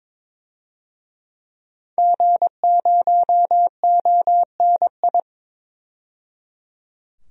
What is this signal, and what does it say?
Morse code, heard as a keyed tone, about 22 wpm, 700 Hz. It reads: G0ONI
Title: DXCC-100 Call Signs - 5 Characters - 22wpm